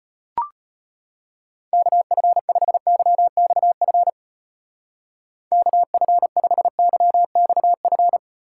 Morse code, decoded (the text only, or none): KF5YXF KF5YXF